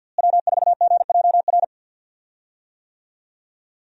Morse code, heard as a keyed tone, about 49 wpm, 700 Hz. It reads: W4GJR
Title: USA Call Signs - 5 Characters - Repeated 3x - 50wpm